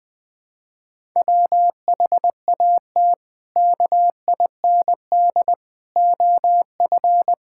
Morse code, heard as a keyed tone, about 20 wpm, 700 Hz. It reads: WHAT KIND OF
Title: Sentences from Top 1000 Words - Review Part 5 - 20wpm